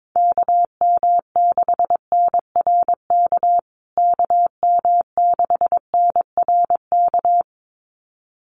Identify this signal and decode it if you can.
Morse code, heard as a keyed tone, about 22 wpm, 700 Hz. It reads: KM6NRK KM6NRK
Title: USA Call Signs Repeated 2x 22wpm